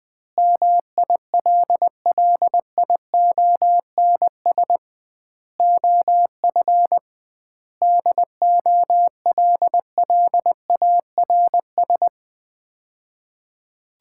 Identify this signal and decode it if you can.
Morse code, heard as a keyed tone, about 21 wpm, 700 Hz. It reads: MILLIONS OF DOLLARS
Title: 2x Word-Spacing - Sets of 5 Words - 20wpm